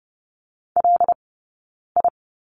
Morse code, heard as a keyed tone, about 30 wpm, 700 Hz. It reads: L I
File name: Lesson 19d - Letter w - Two Characters 30wpm